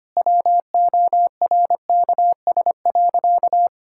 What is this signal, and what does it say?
Morse code, heard as a keyed tone, about 25 wpm, 700 Hz. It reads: WORKS.
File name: practice / 25wpm / Sentences with 2-Word Phrases - 25wpm